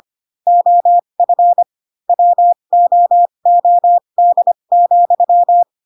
Morse code, heard as a keyed tone, about 25 wpm, spaced as for 22 wpm, 700 Hz. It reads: OF WOOD,